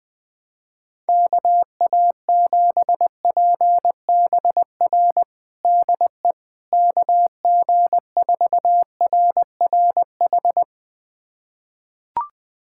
Morse code, heard as a keyed tone, about 20 wpm, 700 Hz. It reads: KA7PBR DE KG4RRH
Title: USA Callsign DE Callsign - 6 Characters - 20wpm